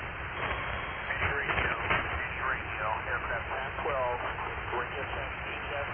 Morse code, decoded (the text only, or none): none